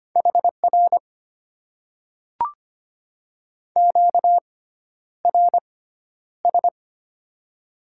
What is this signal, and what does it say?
Morse code, heard as a keyed tone, about 25 wpm, spaced as for 11 wpm, 700 Hz. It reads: HR QRS
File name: Top 90 QSO Elements - 6x Character Spacing - 25wpm